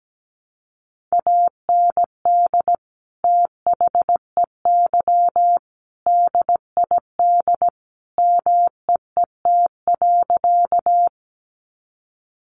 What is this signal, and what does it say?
Morse code, heard as a keyed tone, about 17 wpm, 700 Hz. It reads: AND THEY DID MEET.
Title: Sentences from Top 1100 Words 17wpm